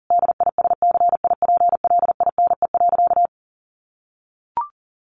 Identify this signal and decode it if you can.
Morse code, heard as a keyed tone, about 40 wpm, 700 Hz. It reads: DISCIPLINE.
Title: Sentences from Top 900 Words 40wpm